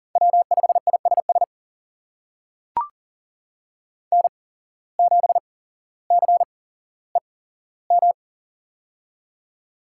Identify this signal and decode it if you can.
Morse code, heard as a keyed tone, about 40 wpm, spaced as for 17 wpm, 700 Hz. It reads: W5ISS N7CEM
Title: USA Call Signs - 5 Characters - 8x Character Spacing - 40wpm